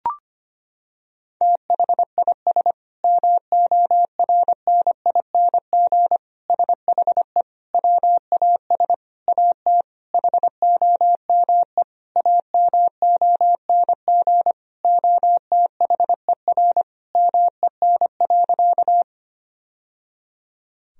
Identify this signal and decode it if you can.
Morse code, heard as a keyed tone, about 25 wpm, 700 Hz. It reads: THIS MORNING SHE WAS AT HOME AMONG OTHER MEN.
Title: Sentences from Top 1000 Words - Review Part 3 - 25wpm